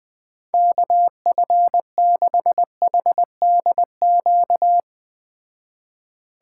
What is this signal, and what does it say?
Morse code, heard as a keyed tone, about 20 wpm, 700 Hz. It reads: KF6HDQ
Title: Speed Racing - USA 2x3 Call Signs 17wpm